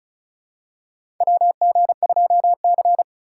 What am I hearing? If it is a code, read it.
Morse code, heard as a keyed tone, about 36 wpm, 700 Hz. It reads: WG2C
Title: USA Call Signs - 4 Characters - 3x Character Spacing - 35wpm